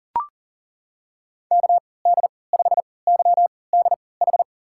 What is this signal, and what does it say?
Morse code, heard as a keyed tone, about 39 wpm, spaced as for 22 wpm, 700 Hz. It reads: KD5YDH
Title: USA Call Signs - 6 Characters - 3x Character Spacing - 40wpm